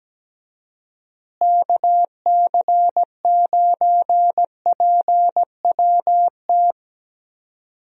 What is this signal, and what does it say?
Morse code, heard as a keyed tone, about 17 wpm, 700 Hz. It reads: KC9PWT
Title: USA Call Signs - 6 Characters - Repeated 2x - 17wpm